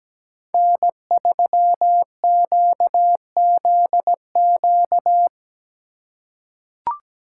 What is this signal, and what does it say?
Morse code, heard as a keyed tone, about 17 wpm, 700 Hz. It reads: N3QZQ
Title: USA Call Signs - 4x Character Spacing - 17wpm